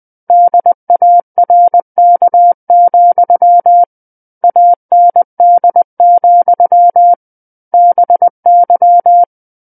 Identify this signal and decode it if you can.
Morse code, heard as a keyed tone, about 20 wpm, 700 Hz. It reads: DARK, AND, BY
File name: 20 WPM